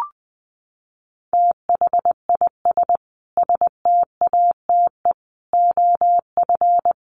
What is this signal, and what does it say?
Morse code, heard as a keyed tone, about 20 wpm, 700 Hz. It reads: THIS STATE OF
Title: Sentences from Top 1100 Words 20wpm